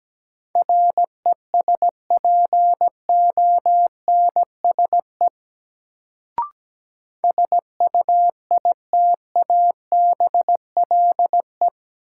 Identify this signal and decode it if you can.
Morse code, heard as a keyed tone, about 17 wpm, 700 Hz. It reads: RESPONSE SUITABLE
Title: Sentences from Top 1400 Words - New Word Drill - 17wpm